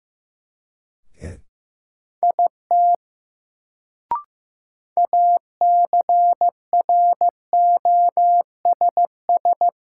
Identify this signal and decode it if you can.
Morse code, heard as a keyed tone, about 15 wpm, 700 Hz. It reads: IT ACROSS
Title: Sets of 2 Words - New Word Drill - 15wpm